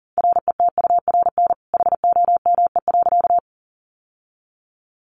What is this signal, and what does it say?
Morse code, heard as a keyed tone, about 40 wpm, 700 Hz. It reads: RETURN HOME.
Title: Sentences from Top 800 Words 40wpm